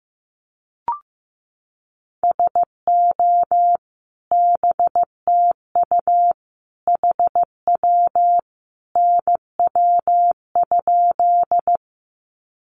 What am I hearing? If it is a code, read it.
Morse code, heard as a keyed tone, about 15 wpm, 700 Hz. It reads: SO BTU HW NW?
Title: Common QSO Phrases 15wpm